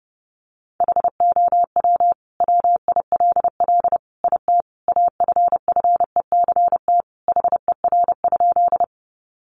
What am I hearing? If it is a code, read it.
Morse code, heard as a keyed tone, about 30 wpm, 700 Hz. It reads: HOW WILL IT AFFECT HER?
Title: Sentences from Top 900 Words 30wpm